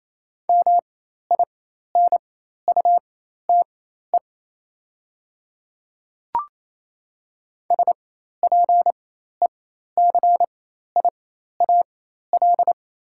Morse code, heard as a keyed tone, about 28 wpm, spaced as for 13 wpm, 690 Hz. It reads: MINUTE SPECIAL
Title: 4x Character Spacing - Top 500 Words - 28wpm